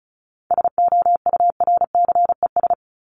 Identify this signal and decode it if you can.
Morse code, heard as a keyed tone, about 34 wpm, 700 Hz. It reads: SOURCES
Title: ES:S Suffix - Top 100 Words with Base Words 35wpm